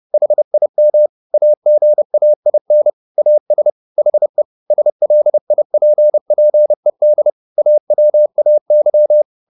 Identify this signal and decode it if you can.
Morse code, heard as a keyed tone, about 30 wpm, 600 Hz. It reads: HIM AGAIN AS HE SLIPPED AWAY